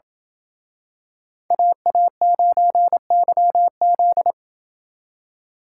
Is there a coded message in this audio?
Yes: Morse code from 1.5 to 4.3 s, heard as a keyed tone, about 28 wpm, 700 Hz. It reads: AA9YZ